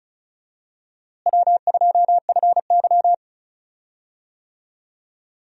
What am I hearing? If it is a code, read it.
Morse code, heard as a keyed tone, about 34 wpm, 700 Hz. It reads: W2FY
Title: USA Call Signs - 4 Characters - Repeated 3x - 35wpm